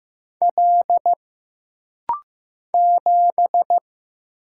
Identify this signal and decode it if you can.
Morse code, heard as a keyed tone, about 15 wpm, 700 Hz. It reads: L 7
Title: Single Letter-Number Alt - Rapid-Fire 15wpm